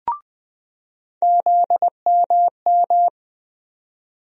Morse code, heard as a keyed tone, about 20 wpm, 700 Hz. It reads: ZMM
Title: USA Call Sign Postfixes - 3 Letters 20wpm